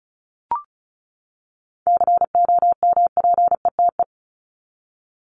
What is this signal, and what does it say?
Morse code, heard as a keyed tone, about 34 wpm, 700 Hz. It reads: COMPETE
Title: Top 1751-2000 Words 35wpm